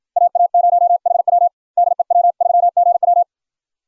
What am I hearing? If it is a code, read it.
Morse code, heard as a keyed tone, about 44 wpm, 700 Hz. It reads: AA0HW DE W3GW